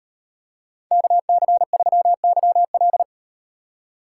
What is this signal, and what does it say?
Morse code, heard as a keyed tone, about 33 wpm, 700 Hz. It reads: KC3YL